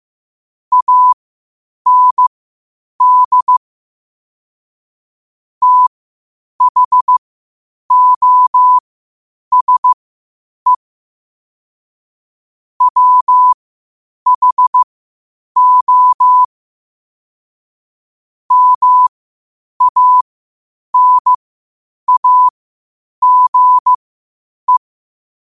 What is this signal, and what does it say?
Morse code, heard as a keyed tone, about 15 wpm, spaced as for 8 wpm, 1000 Hz. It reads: AND THOSE WHO MANAGE
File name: qotdfiles / 8wpm